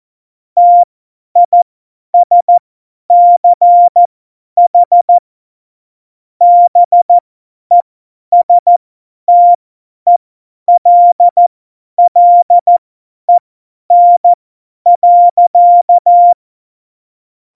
Morse code, heard as a keyed tone, about 14 wpm, spaced as for 10 wpm, 700 Hz. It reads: TISCH BESTELLEN.
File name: Morsekurs / Witze / Witz 03. September 2019, 50 F70